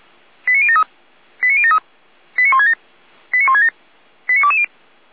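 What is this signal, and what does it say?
Five-tone selective call (ZVEI), heard as quick runs of five tones, tones of about 70 ms. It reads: 89073 89073 89167 89167 89220
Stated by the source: ZVEI five-tone call received on SDR.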